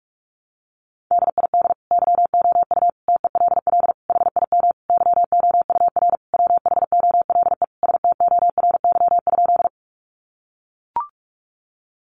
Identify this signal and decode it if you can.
Morse code, heard as a keyed tone, about 44 wpm, 700 Hz. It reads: DID YOU TELL HIM YOUR WHOLE STORY?
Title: Sentences from Top 400 Words 45wpm